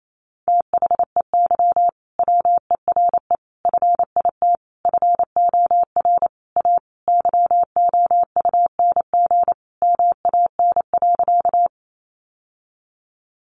Morse code, heard as a keyed tone, about 28 wpm, 700 Hz. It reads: THEY WERE FIT FOR A YOUNG MAN.
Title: Sentences from Top 800 Words 28wpm